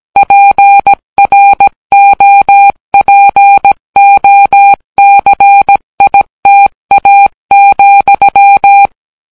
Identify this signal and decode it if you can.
Morse code, heard as a keyed tone, about 17 wpm, 800 Hz. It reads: PROPOCITA,